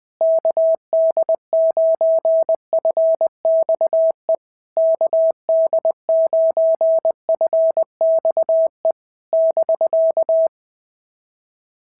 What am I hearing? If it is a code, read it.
Morse code, heard as a keyed tone, about 20 wpm, 650 Hz. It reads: KD9FXE KD9FXE <BK>
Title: SOTA QSO Practice 20wpm